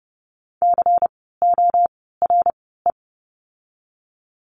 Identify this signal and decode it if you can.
Morse code, heard as a keyed tone, about 30 wpm, spaced as for 17 wpm, 700 Hz. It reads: CORE
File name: Lesson 16f - Letter c - Words 30wpm